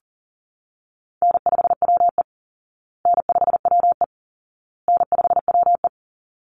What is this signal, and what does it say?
Morse code, heard as a keyed tone, about 37 wpm, 700 Hz. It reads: N5WE N5WE N5WE
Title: Call Signs Repeated 2x - 40wpm